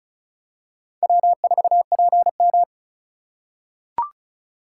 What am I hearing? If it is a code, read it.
Morse code, heard as a keyed tone, about 34 wpm, 700 Hz. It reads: W4PM
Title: USA Call Signs - 4 Characters - 2x Character Spacing - 35wpm